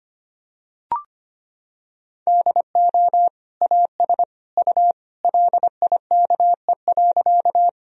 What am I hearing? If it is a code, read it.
Morse code, heard as a keyed tone, about 25 wpm, 700 Hz. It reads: DO AS U LIKE.
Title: Sentences from Top 100 Words and Abbreviations 25wpm